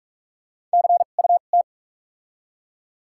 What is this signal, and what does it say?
Morse code, heard as a keyed tone, about 43 wpm, spaced as for 32 wpm, 700 Hz. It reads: CUT
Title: 2x Character Spacing - Top 300 Words - 45wpm